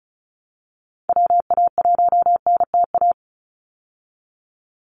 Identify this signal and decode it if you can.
Morse code, heard as a keyed tone, about 34 wpm, 700 Hz. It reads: WA1NTA
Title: Call Signs Repeated 3x - 35wpm